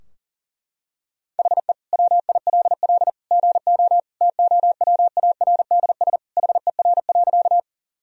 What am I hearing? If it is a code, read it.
Morse code, heard as a keyed tone, about 37 wpm, 700 Hz. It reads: HE WILL GO TOWARDS HER.
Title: Sentences from Top 1000 Words - Review Part 2 - 40wpm